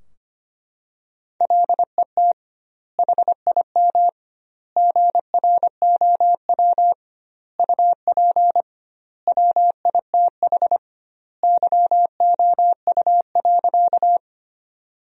Morse code, heard as a keyed tone, about 25 wpm, 700 Hz. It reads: LET HIM GROW UP WITH YOU.
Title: 2x Word-Spacing - Sentences from Top 300 Words - 25wpm